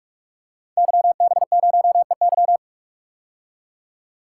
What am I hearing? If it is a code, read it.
Morse code, heard as a keyed tone, about 44 wpm, 700 Hz. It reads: YB0EY
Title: Code Groups - 5 Letter:Numbers - 45wpm